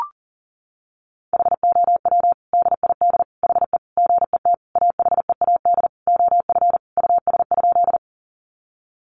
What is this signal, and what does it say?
Morse code, heard as a keyed tone, about 40 wpm, 700 Hz. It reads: HOW DID HE GET AHEAD OF US?
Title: Sentences from Top 1100 Words 40wpm